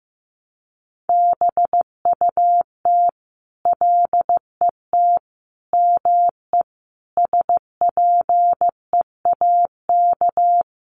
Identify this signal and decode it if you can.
Morse code, heard as a keyed tone, about 15 wpm, 700 Hz. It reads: BUT LET ME SPEAK
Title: Sentences from Top 1100 Words 15wpm